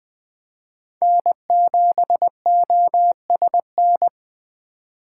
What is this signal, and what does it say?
Morse code, heard as a keyed tone, about 20 wpm, 700 Hz. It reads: N7OSN
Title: USA Call Signs - 4x Character Spacing - 20wpm